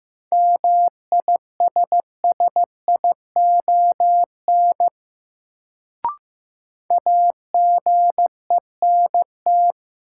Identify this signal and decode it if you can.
Morse code, heard as a keyed tone, about 16 wpm, 700 Hz. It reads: MISSION AGENT
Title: Sentences from Top 1600 Words - New Word Drill - 15wpm